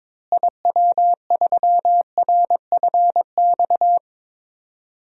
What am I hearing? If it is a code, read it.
Morse code, heard as a keyed tone, about 22 wpm, 700 Hz. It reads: IW3RFX
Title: DXCC-100 Call Signs - 6 Characters - Repeated 2x - 22wpm